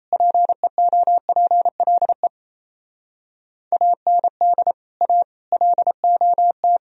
Morse code, heard as a keyed tone, about 37 wpm, 700 Hz. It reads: PEOPLE AND A LOT